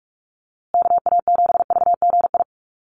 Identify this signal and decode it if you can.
Morse code, heard as a keyed tone, about 44 wpm, 700 Hz. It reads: KA7VGI
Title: USA Call Signs Repeated 2x 45wpm